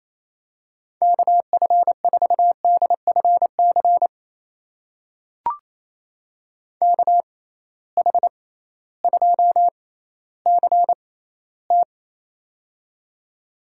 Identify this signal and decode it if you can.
Morse code, heard as a keyed tone, about 28 wpm, spaced as for 15 wpm, 700 Hz. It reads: KF4DFC KH2CT
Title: USA Call Signs - 6x Character Spacing - 28wpm